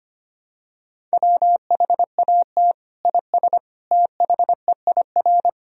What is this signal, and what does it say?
Morse code, heard as a keyed tone, about 25 wpm, 700 Hz. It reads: WHAT IS THEIR
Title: Sentences from Top 1000 Words - Review Part 1 - 25wpm